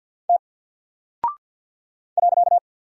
Morse code, heard as a keyed tone, about 48 wpm, 700 Hz. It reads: T .
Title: Single Character (Letter:Number:Punctuation:Prosign) Alt 50wpm